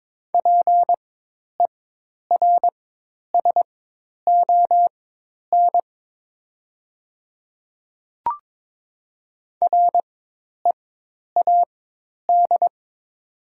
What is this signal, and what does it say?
Morse code, heard as a keyed tone, about 22 wpm, spaced as for 10 wpm, 700 Hz. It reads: PERSON READ
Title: 4x Character Spacing - Top 500 Words - 22wpm